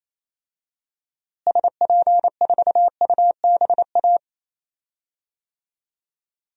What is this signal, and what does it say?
Morse code, heard as a keyed tone, about 28 wpm, 700 Hz. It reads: SP4UBA
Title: DXCC-100 Call Signs - 6 Characters - Repeated 2x - 28wpm